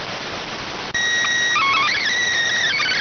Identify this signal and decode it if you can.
SSTV (slow-scan television), PD 290.